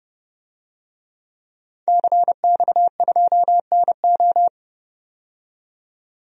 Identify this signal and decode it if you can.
Morse code, heard as a keyed tone, about 30 wpm, 700 Hz. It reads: CX2NO
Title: DXCC-100 Call Signs - 5 Characters - 30wpm